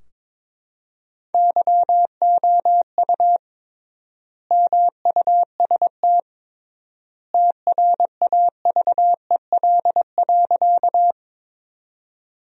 Morse code, heard as a keyed tone, about 22 wpm, 700 Hz. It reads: YOU MUST TRAVEL.
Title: 3x Word-Spacing - Sentences from Top 400 Words - 22wpm